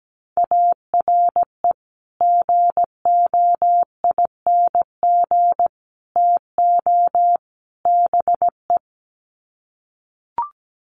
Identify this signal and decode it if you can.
Morse code, heard as a keyed tone, about 17 wpm, 700 Hz. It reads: ARE GOING TO BE
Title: Sets of 4 Words 17wpm